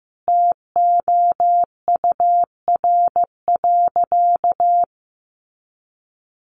Morse code, heard as a keyed tone, about 15 wpm, 700 Hz. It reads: TOUR.
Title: Sentences from Top 1400 Words 15wpm